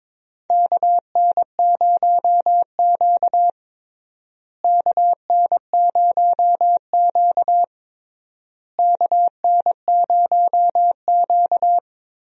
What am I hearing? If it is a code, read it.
Morse code, heard as a keyed tone, about 22 wpm, 700 Hz. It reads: KN0Q KN0Q KN0Q
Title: USA Call Signs - 4 Characters - Repeated 3x - 22wpm